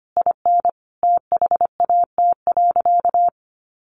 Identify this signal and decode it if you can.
Morse code, heard as a keyed tone, about 25 wpm, 700 Hz. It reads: IN THAT.
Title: Sentences from Top 200 Words 25wpm